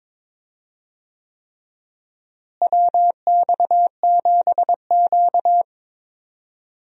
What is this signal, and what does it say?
Morse code, heard as a keyed tone, about 22 wpm, 700 Hz. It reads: WX7Q
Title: USA Call Signs - 4 Characters - Repeated 3x - 22wpm